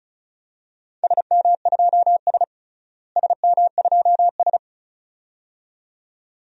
Morse code, heard as a keyed tone, about 36 wpm, 700 Hz. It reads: SM2S SM2S
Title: DXCC-100 Call Signs - 4 Characters - Repeated 2x - 35wpm